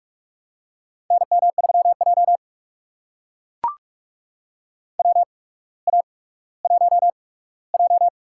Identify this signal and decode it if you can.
Morse code, heard as a keyed tone, about 44 wpm, spaced as for 19 wpm, 700 Hz. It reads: NM3J WA1J
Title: USA Call Signs - 4 Characters - 8x Character Spacing - 45wpm